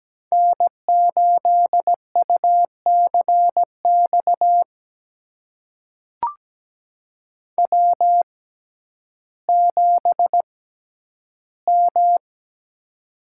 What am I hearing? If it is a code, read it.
Morse code, heard as a keyed tone, about 17 wpm, spaced as for 10 wpm, 700 Hz. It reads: N8UCX W7M
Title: USA Call Signs - 5 Characters - 6x Character Spacing - 17wpm